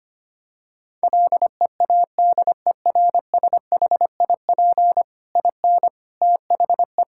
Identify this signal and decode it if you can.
Morse code, heard as a keyed tone, about 25 wpm, 700 Hz. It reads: LEADERSHIP IN THE
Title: Sentences from Top 1600 Words 25wpm